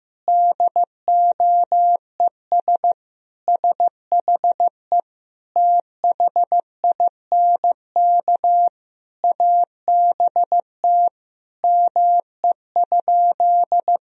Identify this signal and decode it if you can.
Morse code, heard as a keyed tone, about 15 wpm, 700 Hz. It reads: DOES SHE THINK ABT ME?
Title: Sentences from Top 200 Words and Abbreviations 15wpm